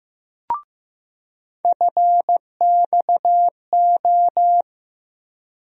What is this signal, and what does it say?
Morse code, heard as a keyed tone, about 15 wpm, 700 Hz. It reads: FXO
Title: USA Call Sign Postfixes - 3 Letters 15wpm